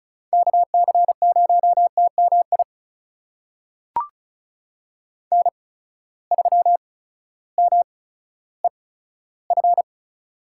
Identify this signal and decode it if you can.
Morse code, heard as a keyed tone, about 36 wpm, spaced as for 16 wpm, 700 Hz. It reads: KC0TMI N3MEF
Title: USA Call Signs - 8x Character Spacing - 35wpm